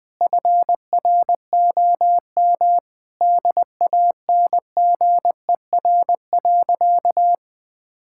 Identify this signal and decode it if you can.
Morse code, heard as a keyed tone, about 20 wpm, 700 Hz. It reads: FROM DANGER.
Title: Sentences with 2-Word Phrases - 20wpm